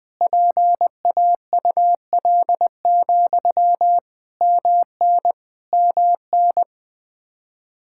Morse code, heard as a keyed tone, about 20 wpm, 700 Hz. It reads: PAUL, MN MN
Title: Practice QSO - Exchange 1 - Beginning - 20wpm